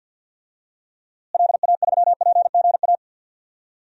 Morse code, heard as a keyed tone, about 52 wpm, 700 Hz. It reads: LA3PGA
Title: DXCC-100 Call Signs - 6 Characters - Repeated 3x - 50wpm